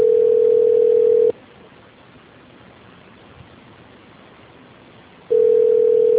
Ringback tone (North America), 440 + 480 Hz.
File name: out- -2033- - - .44